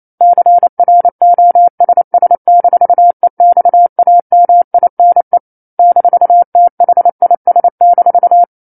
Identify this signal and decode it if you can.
Morse code, heard as a keyed tone, about 28 wpm, 700 Hz. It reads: CROSS-EXAMINE -THIS-
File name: 09-Chapter-09 28 WPM